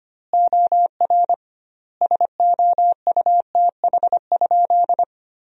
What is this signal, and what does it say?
Morse code, heard as a keyed tone, about 25 wpm, 700 Hz. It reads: OR SOUTH?
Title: 2x Word-Spacing - Sentences from Top 400 Words - 25wpm